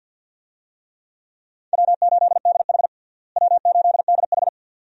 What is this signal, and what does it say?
Morse code, heard as a keyed tone, about 50 wpm, 700 Hz. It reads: W8DH W8DH
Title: USA Call Signs - 4 Characters - Repeated 2x - 50wpm